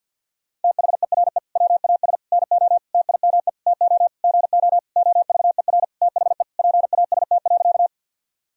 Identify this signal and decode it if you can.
Morse code, heard as a keyed tone, about 50 wpm, 700 Hz. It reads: THERE WAS NO TIME TO GO OVER THE PAST.
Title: Sentences from Top 1000 Words - Review Part 3 - 50wpm